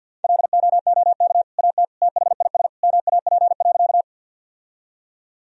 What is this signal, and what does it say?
Morse code, heard as a keyed tone, about 50 wpm, 700 Hz. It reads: LOOK AT THIS MAP.
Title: Sentences from Top 1000 Words - Review Part 2 - 50wpm